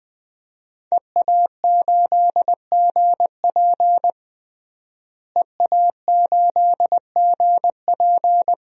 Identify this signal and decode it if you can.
Morse code, heard as a keyed tone, about 20 wpm, 700 Hz. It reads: EA8GP EA8GP
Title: DXCC-100 Call Signs - 5 Characters - Repeated 2x - 20wpm